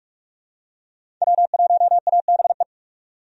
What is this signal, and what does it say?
Morse code, heard as a keyed tone, about 43 wpm, 700 Hz. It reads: W1ABE
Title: USA Call Signs - 5 Characters - 45wpm